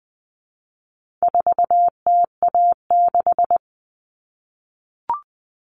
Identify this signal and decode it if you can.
Morse code, heard as a keyed tone, about 20 wpm, 700 Hz. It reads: 4TA6
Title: All Cut Numbers - 4-Digit 20wpm